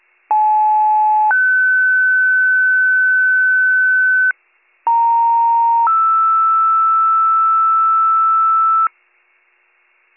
Two-tone page, heard as a one-second tone, then a three-second tone, about 860 then 1500 Hz, about 930 then 1320 Hz. It is heard from 0.3 to 8.9 s.